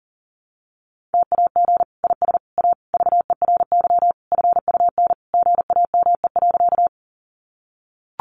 Morse code, heard as a keyed tone, about 40 wpm, 700 Hz. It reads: TAG IS A VERY FUN GAME.
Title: Additional Easy Phrases 40wpm